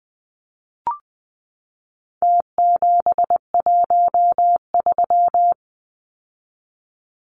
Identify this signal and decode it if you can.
Morse code, heard as a keyed tone, about 20 wpm, 700 Hz. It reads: T713
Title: Numbers - Contest Style - 4-Digit 20wpm